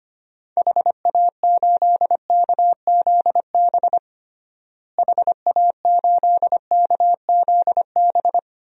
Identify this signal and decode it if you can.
Morse code, heard as a keyed tone, about 25 wpm, 700 Hz. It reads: HA8KZB HA8KZB
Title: DXCC-100 Call Signs - 6 Characters - Repeated 3x - 25wpm